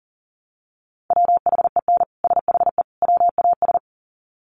Morse code, heard as a keyed tone, about 40 wpm, 700 Hz. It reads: WHEN SHE WAS
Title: Sets of 3 Words 40wpm